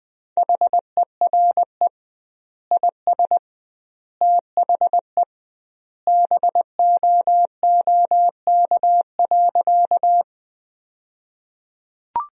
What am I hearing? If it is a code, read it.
Morse code, heard as a keyed tone, about 20 wpm, 700 Hz. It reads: HERE IS THE BOOK.
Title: 2x Word-Spacing - Sentences from Top 300 Words - 20wpm